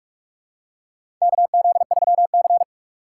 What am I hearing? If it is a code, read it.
Morse code, heard as a keyed tone, about 44 wpm, 700 Hz. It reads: KZ3C
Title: USA Call Signs - 2x Character Spacing - 45wpm